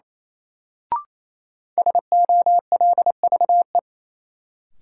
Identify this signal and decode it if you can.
Morse code, heard as a keyed tone, about 28 wpm, 700 Hz. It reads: SOLVE
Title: Sentences from Top 1000 Words - New Word Drill - 28wpm